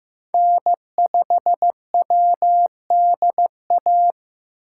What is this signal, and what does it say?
Morse code, heard as a keyed tone, about 15 wpm, 700 Hz. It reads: N5WDA
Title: USA Call Signs - 5 Characters - 2x Character Spacing - 15wpm